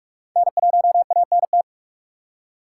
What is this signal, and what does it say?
Morse code, heard as a keyed tone, about 46 wpm, 700 Hz. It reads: N1ANT
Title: USA Call Signs - 5 Characters - 3x Character Spacing - 45wpm